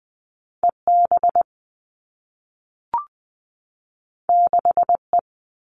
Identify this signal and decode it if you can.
Morse code, heard as a keyed tone, about 20 wpm, 700 Hz. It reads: EB 6E
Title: All Cut Numbers - 2-Digit 20wpm